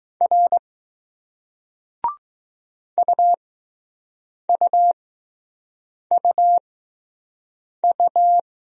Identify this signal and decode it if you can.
Morse code, heard as a keyed tone, about 20 wpm, spaced as for 6 wpm, 700 Hz. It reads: R UUUU